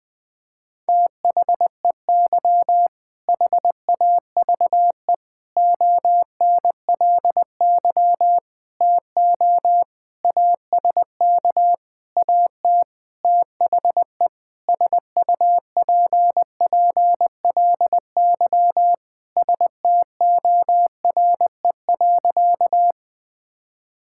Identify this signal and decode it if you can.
Morse code, heard as a keyed tone, about 21 wpm, 700 Hz. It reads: THEY HAVE ONLY TO ASK AT THE SUPPLY STORE.
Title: Sentences from Top 1000 Words - Review Part 5 - 20wpm